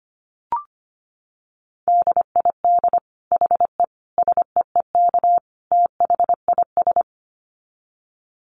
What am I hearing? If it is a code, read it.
Morse code, heard as a keyed tone, about 25 wpm, 700 Hz. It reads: DID HE SEEK THIS
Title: Dit Challenge 25wpm